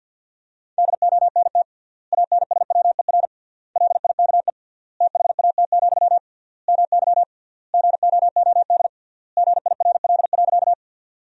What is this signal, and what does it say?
Morse code, heard as a keyed tone, about 50 wpm, 700 Hz. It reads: DONT ANSWER LIKE THAT, MY GOOD GIRL.
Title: 3x Word-Spacing - Sentences from Top 400 Words - 50wpm